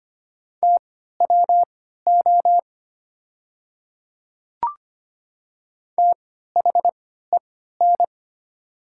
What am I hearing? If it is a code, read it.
Morse code, heard as a keyed tone, about 25 wpm, spaced as for 14 wpm, 700 Hz. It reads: TWO THEN
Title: 3x Character Spacing - Top 200 Words - 25wpm